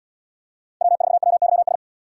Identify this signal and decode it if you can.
Morse code, heard as a keyed tone, about 67 wpm, 700 Hz. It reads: W4WYS